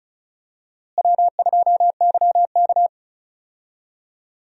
Morse code, heard as a keyed tone, about 36 wpm, 700 Hz. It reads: W2YK
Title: USA Call Signs - 4 Characters - Repeated 2x - 35wpm